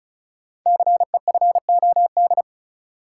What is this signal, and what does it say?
Morse code, heard as a keyed tone, about 36 wpm, 700 Hz. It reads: CEFOD
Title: Code Groups - 5 Letters - 35wpm